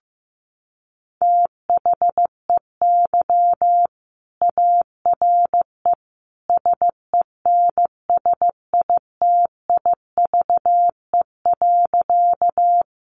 Morse code, heard as a keyed tone, about 15 wpm, 700 Hz. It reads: THEY ARE SENSITIVE.
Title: Sentences from Top 1400 Words 15wpm